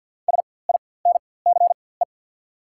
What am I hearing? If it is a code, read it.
Morse code, heard as a keyed tone, about 53 wpm, spaced as for 24 wpm, 690 Hz. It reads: SINCE
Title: 4x Character Spacing - Top 300 Words - 50wpm